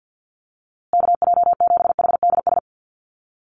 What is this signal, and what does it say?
Morse code, heard as a keyed tone, about 50 wpm, 700 Hz. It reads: KJ7HDS
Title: USA Call Signs Repeated 3x 50wpm